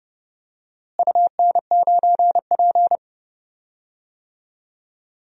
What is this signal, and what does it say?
Morse code, heard as a keyed tone, about 30 wpm, 700 Hz. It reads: UN9P
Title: DXCC-100 Call Signs - 4 Characters - Repeated 2x - 30wpm